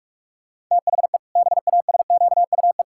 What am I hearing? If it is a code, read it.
Morse code, heard as a keyed tone, about 46 wpm, 700 Hz. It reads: THE BASQUE
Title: Pangram Challenge v2 - 45wpm